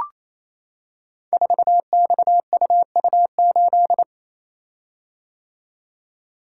Morse code, heard as a keyed tone, about 27 wpm, 700 Hz. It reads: 4XUU8
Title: Code Groups - 5 Letter:Numbers - 28wpm